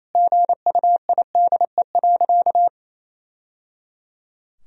Morse code, heard as a keyed tone, about 28 wpm, 700 Hz. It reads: GUIDE.
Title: Sentences from Top 1000 Words - Review Part 3 - 28wpm